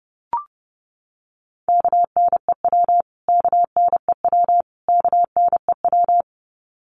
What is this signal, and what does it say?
Morse code, heard as a keyed tone, about 30 wpm, 700 Hz. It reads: KNEW KNEW KNEW
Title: Top 500 Words Repeated 30wpm